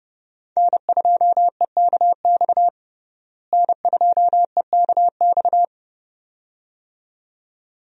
Morse code, heard as a keyed tone, about 32 wpm, 700 Hz. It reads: N2EKX N2EKX
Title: USA Call Signs - 5 Characters - Repeated 2x - 30wpm